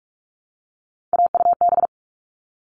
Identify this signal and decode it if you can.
Morse code, heard as a keyed tone, about 44 wpm, 700 Hz. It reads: AUB
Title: All Cut Numbers - 3-Digit 45wpm